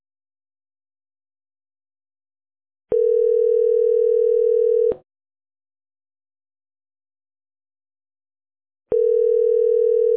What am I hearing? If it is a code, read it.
Ringback tone (North America), 440 + 480 Hz.